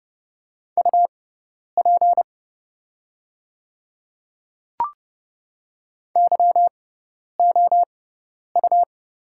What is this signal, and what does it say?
Morse code, heard as a keyed tone, about 29 wpm, spaced as for 10 wpm, 690 Hz. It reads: UP YOU
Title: 6x Character Spacing - Top 100 Words - 30wpm